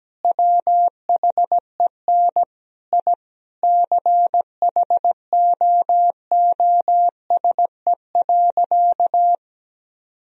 Morse code, heard as a keyed tone, about 17 wpm, 700 Hz. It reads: WHEN I CHOOSE.
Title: Sentences from Top 1000 Words - Review Part 4 - 17wpm